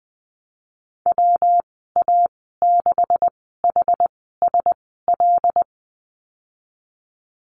Morse code, heard as a keyed tone, about 20 wpm, spaced as for 15 wpm, 700 Hz. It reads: WA6HSL
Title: 161 Lesson 26g - Letter g - Call Signs 20wpm